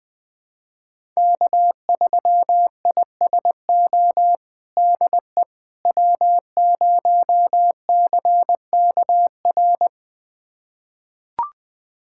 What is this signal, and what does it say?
Morse code, heard as a keyed tone, about 21 wpm, 700 Hz. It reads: K3ISO DE W0CKR
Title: USA Callsign DE Callsign - 5 Characters - 20wpm